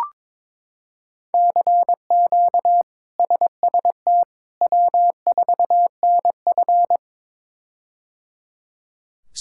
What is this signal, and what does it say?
Morse code, heard as a keyed tone, about 22 wpm, 700 Hz. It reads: CQ SST W4NF
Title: K1USN SST QSO Practice - 22wpm